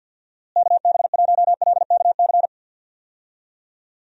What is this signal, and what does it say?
Morse code, heard as a keyed tone, about 45 wpm, 700 Hz. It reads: KB1LKC
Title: USA Call Signs - 6 Characters - Repeated 2x - 50wpm